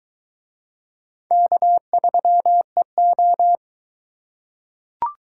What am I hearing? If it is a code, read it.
Morse code, heard as a keyed tone, about 22 wpm, 700 Hz. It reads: K3EO